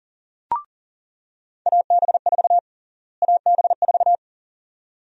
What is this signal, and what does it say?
Morse code, heard as a keyed tone, about 40 wpm, 700 Hz. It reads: AB4 AB4
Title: USA Call Sign Prefixes - Repeated 2x - 40wpm